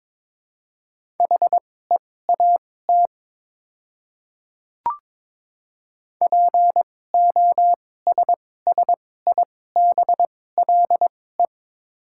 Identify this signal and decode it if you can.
Morse code, heard as a keyed tone, about 22 wpm, spaced as for 16 wpm, 700 Hz. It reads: HEAT POSSIBLE
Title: 2x Character Spacing - Top 500 Words - 22wpm